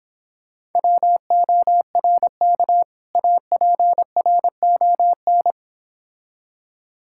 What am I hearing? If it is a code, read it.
Morse code, heard as a keyed tone, about 26 wpm, 700 Hz. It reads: WORK APRON